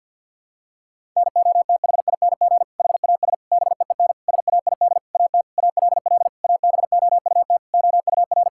Morse code, heard as a keyed tone, about 51 wpm, 700 Hz. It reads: NOTHING HAS BEEN SAID AT ALL ABOUT OUR
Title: Sentences from Top 1000 Words - Review Part 5 - 50wpm